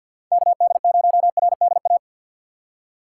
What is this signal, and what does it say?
Morse code, heard as a keyed tone, about 53 wpm, 700 Hz. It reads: KD0RDA
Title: USA Call Signs - 2x Character Spacing - 50wpm